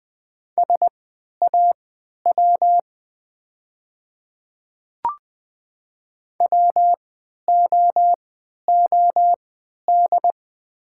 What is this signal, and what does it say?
Morse code, heard as a keyed tone, about 20 wpm, spaced as for 11 wpm, 700 Hz. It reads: SAW WOOD
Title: 3x Character Spacing - Top 500 Words - 20wpm